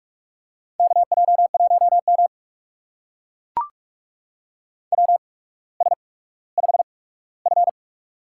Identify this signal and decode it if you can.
Morse code, heard as a keyed tone, about 44 wpm, spaced as for 19 wpm, 700 Hz. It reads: KJ1M WS5F
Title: USA Call Signs - 4 Characters - 8x Character Spacing - 45wpm